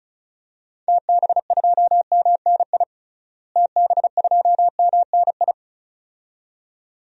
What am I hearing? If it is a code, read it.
Morse code, heard as a keyed tone, about 34 wpm, 700 Hz. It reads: TB2MNI TB2MNI
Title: DXCC-100 Call Signs - 6 Characters - Repeated 2x - 35wpm